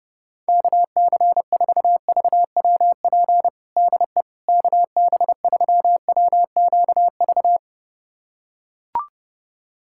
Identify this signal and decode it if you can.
Morse code, heard as a keyed tone, about 30 wpm, 700 Hz. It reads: KC4VWP DE KB3WQV
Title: USA Callsign DE Callsign 30wpm